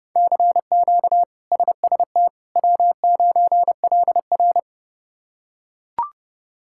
Morse code, heard as a keyed tone, about 32 wpm, 700 Hz. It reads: CQ SST W9LR
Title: K1USN SST QSO Practice - 30wpm